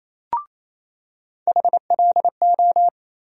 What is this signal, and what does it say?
Morse code, heard as a keyed tone, about 28 wpm, 700 Hz. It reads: HLO
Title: USA Call Sign Postfixes - 3 Letters 28wpm